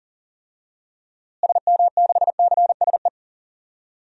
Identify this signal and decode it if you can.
Morse code, heard as a keyed tone, about 37 wpm, 700 Hz. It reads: SM6CSE
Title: DXCC-100 Call Signs - 6 Characters - Repeated 2x - 40wpm